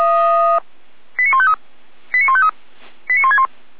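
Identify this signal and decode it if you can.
Five-tone selective call (ZVEI), heard as quick runs of five tones, tones of about 70 ms. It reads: 89153 89153 89161
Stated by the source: ZVEI five-tone call received by SDR.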